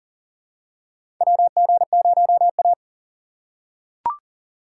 Morse code, heard as a keyed tone, about 40 wpm, 700 Hz. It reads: WG0A
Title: USA Call Signs - 4 Characters - 6x Character Spacing - 40wpm